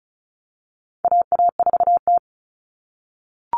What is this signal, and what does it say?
Morse code, heard as a keyed tone, about 34 wpm, 700 Hz. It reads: AA4T
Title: All Cut Numbers - 4-Digit 35wpm